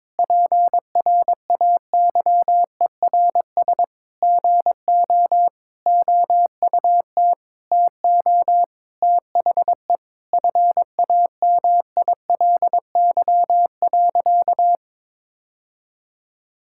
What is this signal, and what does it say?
Morse code, heard as a keyed tone, about 22 wpm, 700 Hz. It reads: PRAYERS GO OUT TO THE FAMILY.
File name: practice / 22wpm / Sentences with 8-Word Phrases - 22wpm